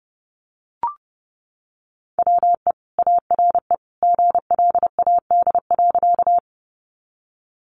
Morse code, heard as a keyed tone, about 30 wpm, 700 Hz. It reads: WE ARE GLAD.
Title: Sentences from Top 800 Words 30wpm